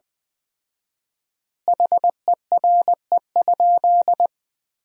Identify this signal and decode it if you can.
Morse code, heard as a keyed tone, about 20 wpm, 700 Hz. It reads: HERE?
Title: 4x Word-Spacing - Sentences from Top 200 Words - 20wpm